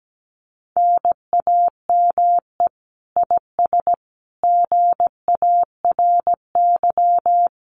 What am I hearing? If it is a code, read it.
Morse code, heard as a keyed tone, about 17 wpm, 700 Hz. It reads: NAME IS GARY
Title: Common QSO Phrases Encore 17wpm